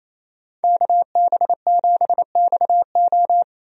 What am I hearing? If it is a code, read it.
Morse code, heard as a keyed tone, about 27 wpm, 700 Hz. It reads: KB7XO
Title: USA Call Signs - 5 Characters - Repeated 3x - 28wpm